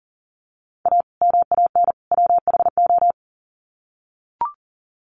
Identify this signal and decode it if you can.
Morse code, heard as a keyed tone, about 40 wpm, 700 Hz. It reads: A MAN WHO
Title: Sets of 3 Words 40wpm